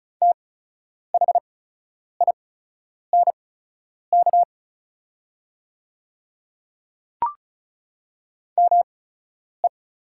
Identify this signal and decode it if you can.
Morse code, heard as a keyed tone, about 36 wpm, spaced as for 10 wpm, 700 Hz. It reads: THINK ME
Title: 8x Character Spacing - Top 300 Words - 35wpm